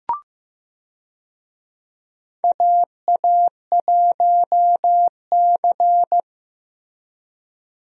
Morse code, heard as a keyed tone, about 15 wpm, 700 Hz. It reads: AA1C
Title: USA Call Signs - 4 Characters - Repeated 2x - 15wpm